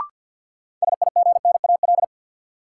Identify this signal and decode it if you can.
Morse code, heard as a keyed tone, about 53 wpm, 700 Hz. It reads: SIGNAL
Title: Sentences from Top 1400 Words - New Word Drill Alt - 50wpm